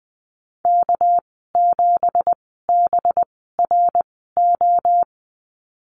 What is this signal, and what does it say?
Morse code, heard as a keyed tone, about 20 wpm, spaced as for 14 wpm, 700 Hz. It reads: K7BRO
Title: 245 Lesson 39g - Letter z - Call Signs 20wpm